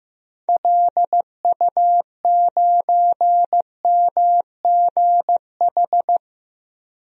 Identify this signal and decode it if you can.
Morse code, heard as a keyed tone, about 15 wpm, 700 Hz. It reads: LU9MGH
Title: DXCC-100 Call Signs - 6 Characters - 15wpm